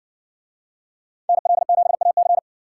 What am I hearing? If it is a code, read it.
Morse code, heard as a keyed tone, about 60 wpm, 700 Hz. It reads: NL7AC